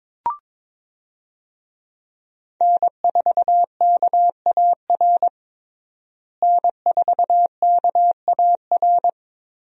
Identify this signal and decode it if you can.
Morse code, heard as a keyed tone, about 22 wpm, 700 Hz. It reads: N4KAR N4KAR
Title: USA Call Signs - 5 Characters - Repeated 2x - 22wpm